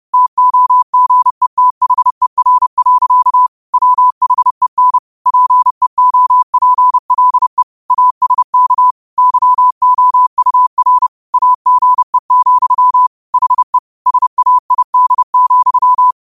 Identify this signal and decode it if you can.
Morse code, heard as a keyed tone, about 30 wpm, 1000 Hz. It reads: TOGETHER. WHEN PEOPLE ASK YOUR AGE, HE SAID,
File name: qotdfiles / 30wpm